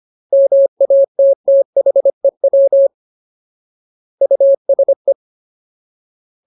Morse code, heard as a keyed tone, about 25 wpm, 550 Hz. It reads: MATTHEW USE
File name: Ultimate Word List / 25WPM Ultimate